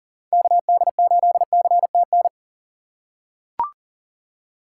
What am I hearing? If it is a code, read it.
Morse code, heard as a keyed tone, about 40 wpm, 700 Hz. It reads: KD8CTN
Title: USA Call Signs - 6 Characters - 4x Character Spacing - 40wpm